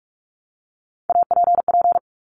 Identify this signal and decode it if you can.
Morse code, heard as a keyed tone, about 46 wpm, 700 Hz. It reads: APP
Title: Top Three Letter Combinations in Words 45wpm